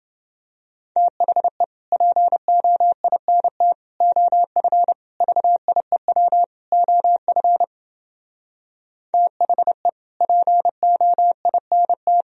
Morse code, heard as a keyed tone, about 30 wpm, 700 Hz. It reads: THE POINT OF VIEW OF THE POINT